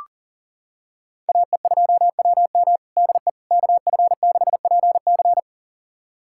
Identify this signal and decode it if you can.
Morse code, heard as a keyed tone, about 40 wpm, 700 Hz. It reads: AE2WM DE KF6PC
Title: USA Callsign DE Callsign - 5 Characters - 40wpm